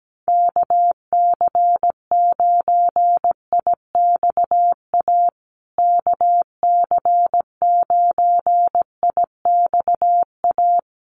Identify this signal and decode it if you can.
Morse code, heard as a keyed tone, about 17 wpm, 700 Hz. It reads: KC9IXA KC9IXA
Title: USA Call Signs Repeated 3x 17wpm